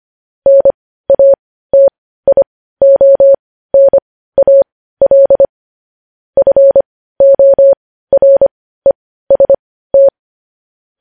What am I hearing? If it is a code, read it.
Morse code, heard as a keyed tone, about 25 wpm, spaced as for 15 wpm, 550 Hz. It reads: NATIONAL FOREST
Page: news headlines in morse code at 15 wpm